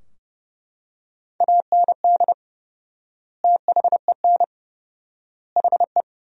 Morse code, heard as a keyed tone, about 32 wpm, 700 Hz. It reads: AND THEN HE
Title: 4x Word-Spacing - Sentences from Top 400 Words - 30wpm